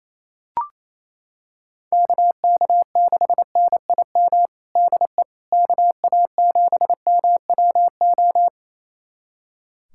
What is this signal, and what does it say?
Morse code, heard as a keyed tone, about 28 wpm, 700 Hz. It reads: KK6NIM DE KA7MWO
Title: USA Callsign DE Callsign - 6 Characters - 28wpm